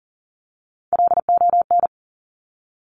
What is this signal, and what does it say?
Morse code, heard as a keyed tone, about 42 wpm, 700 Hz. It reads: LON
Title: Top Three Letter Combinations in Words 40wpm